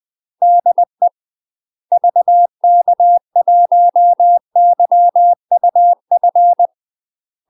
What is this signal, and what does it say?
Morse code, heard as a keyed tone, about 20 wpm, 700 Hz. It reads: DE VK1YUF
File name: UK MASTER QSOs TRAINING / 20WPM / UK MASTER QSOs 0022